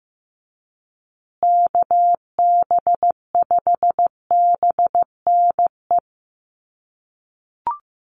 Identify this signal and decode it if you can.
Morse code, heard as a keyed tone, about 15 wpm, 700 Hz. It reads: KB5BNE
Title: USA Call Signs Repeated 3x 15wpm